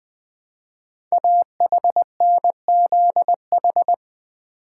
Morse code, heard as a keyed tone, about 20 wpm, 700 Hz. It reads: AHNZH
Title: Code Groups - 5 Letters - 20wpm